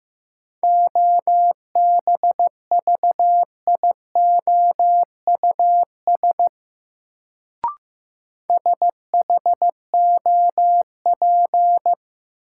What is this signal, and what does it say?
Morse code, heard as a keyed tone, about 15 wpm, 700 Hz. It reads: OBVIOUS SHOP
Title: Sentences from Top 1600 Words - New Word Drill - 15wpm